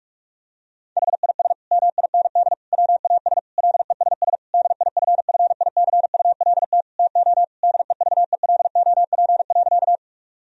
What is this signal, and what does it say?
Morse code, heard as a keyed tone, about 41 wpm, 700 Hz. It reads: HIS MIND WAS LESS DIFFICULT TO DEVELOP.
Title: Sentences from Top 1000 Words - Review Part 2 - 45wpm